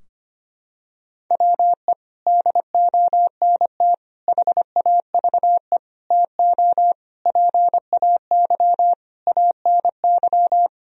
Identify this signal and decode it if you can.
Morse code, heard as a keyed tone, about 25 wpm, 700 Hz. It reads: WE DONT HAVE TO PAY ANY
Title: Sentences from Top 1000 Words - Review Part 2 - 25wpm